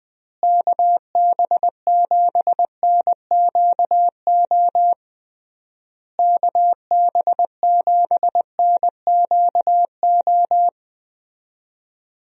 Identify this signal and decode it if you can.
Morse code, heard as a keyed tone, about 20 wpm, 700 Hz. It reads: KB7NQO KB7NQO
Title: USA Call Signs - 6 Characters - Repeated 3x - 20wpm